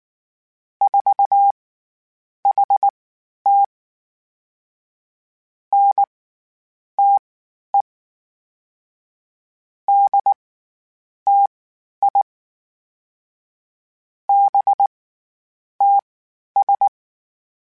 Morse code, heard as a keyed tone, about 19 wpm, spaced as for 9 wpm, 800 Hz. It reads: 4HT NTE DTI BTS
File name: CW Comparisons / CW Comparisons-mp3s / CW Comparisons Broken Characters @18